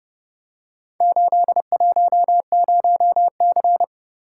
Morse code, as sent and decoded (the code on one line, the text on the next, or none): ---.. .---- ----- -.-.
810C